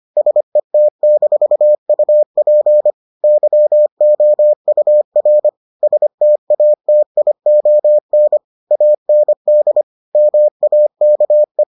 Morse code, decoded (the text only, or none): SET-UP YOUR STATION AND MAKE